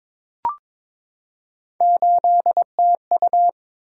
Morse code, heard as a keyed tone, about 22 wpm, 700 Hz. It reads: 8TU
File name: practice / 22wpm / Three Letter-Number Combinations - 22wpm